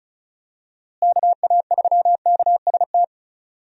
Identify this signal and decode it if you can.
Morse code, heard as a keyed tone, about 34 wpm, 700 Hz. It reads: KA3KST
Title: USA Call Signs - 6 Characters - Repeated 3x - 35wpm